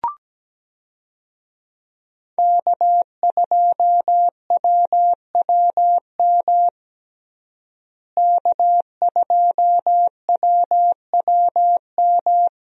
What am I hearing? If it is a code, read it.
Morse code, heard as a keyed tone, about 17 wpm, 700 Hz. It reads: K2WWM K2WWM
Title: USA Call Signs - 5 Characters - Repeated 2x - 17wpm